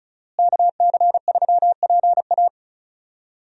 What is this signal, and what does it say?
Morse code, heard as a keyed tone, about 34 wpm, 700 Hz. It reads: KC3PA
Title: USA Call Signs - 5 Characters - Repeated 3x - 35wpm